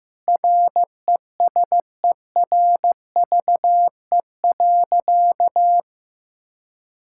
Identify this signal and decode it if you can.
Morse code, heard as a keyed tone, about 16 wpm, 700 Hz. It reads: RESERVE.
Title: Sentences from Top 1000 Words - Review Part 6 - 15wpm